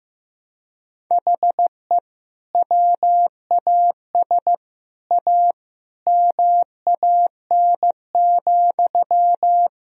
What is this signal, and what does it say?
Morse code, heard as a keyed tone, about 15 wpm, 700 Hz. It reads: HE WAS A MAN,
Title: Sentences from Top 1000 Words - Review Part 3 - 15wpm